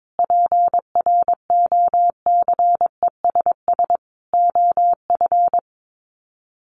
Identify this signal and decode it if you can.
Morse code, heard as a keyed tone, about 22 wpm, 700 Hz. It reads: PROCESS OF
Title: Sets of 5 Words Encore2 22wpm